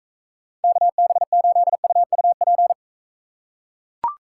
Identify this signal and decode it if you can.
Morse code, heard as a keyed tone, about 42 wpm, 700 Hz. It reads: KB8UUP